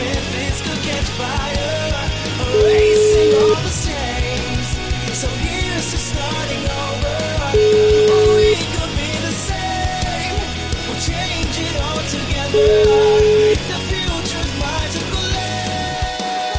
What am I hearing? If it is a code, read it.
Ringback tone (Europe, 425 Hz), with noise.